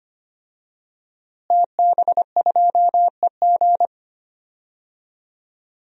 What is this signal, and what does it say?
Morse code, heard as a keyed tone, about 25 wpm, 700 Hz. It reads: TB2EG
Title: DXCC-100 Call Signs - 5 Characters - Repeated 3x - 25wpm